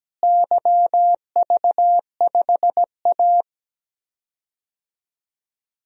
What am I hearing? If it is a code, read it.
Morse code, heard as a keyed tone, about 17 wpm, 700 Hz. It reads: YV5A
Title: DXCC-100 Call Signs - 4 Characters - Repeated 2x - 17wpm